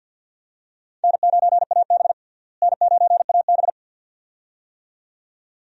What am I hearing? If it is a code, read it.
Morse code, heard as a keyed tone, about 52 wpm, 700 Hz. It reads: N9AB N9AB
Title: USA Call Signs - 4 Characters - Repeated 2x - 50wpm